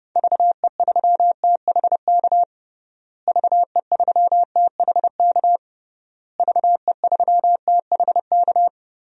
Morse code, heard as a keyed tone, about 30 wpm, 700 Hz. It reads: VE3THK VE3THK VE3THK
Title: DXCC-100 Call Signs - 6 Characters - Repeated 3x - 30wpm